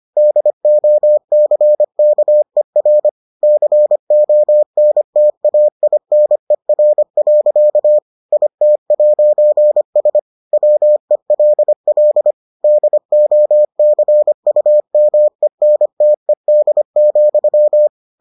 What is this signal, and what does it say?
Morse code, heard as a keyed tone, about 25 wpm, 600 Hz. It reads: DOCKER CONTAINER. IT'S WELL DOCUMENTED,